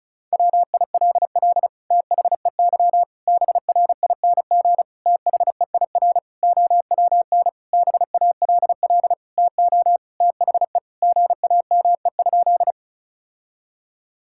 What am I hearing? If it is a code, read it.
Morse code, heard as a keyed tone, about 34 wpm, 700 Hz. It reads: WILL THEY BRING THEIR OWN BALL TO THE GAME?
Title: Sentences from Top 1000 Words - Review Part 6 - 35wpm